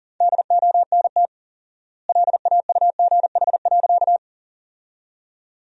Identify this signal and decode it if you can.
Morse code, heard as a keyed tone, about 37 wpm, 700 Hz. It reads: DONT LAUGH.
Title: 4x Word-Spacing - Sentences from Top 500 Words - 40wpm